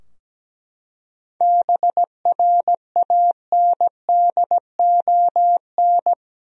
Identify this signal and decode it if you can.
Morse code, heard as a keyed tone, about 17 wpm, 700 Hz. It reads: BRANDON
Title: USA Common Names - 17wpm